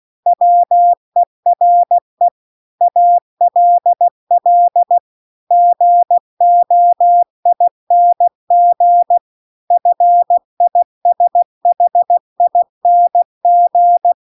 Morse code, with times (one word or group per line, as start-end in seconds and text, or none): 0.3-2.3 WERE
2.8-5.0 ALL
5.5-9.2 GOING
9.7-14.1 FISHING